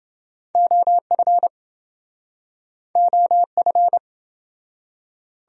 Morse code, heard as a keyed tone, about 27 wpm, 700 Hz. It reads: OF OF